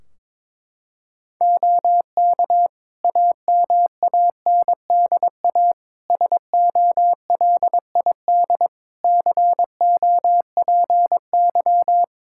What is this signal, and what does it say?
Morse code, heard as a keyed tone, about 22 wpm, 700 Hz. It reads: OK AMANDA SOLID COPY
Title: K1USN SST QSO Practice 22wpm